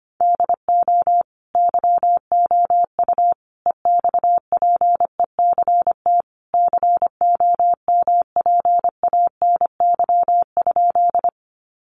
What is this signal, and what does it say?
Morse code, heard as a keyed tone, about 25 wpm, 700 Hz. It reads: DO YOU EXPECT COMPANY?
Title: Sentences from Top 1100 Words 25wpm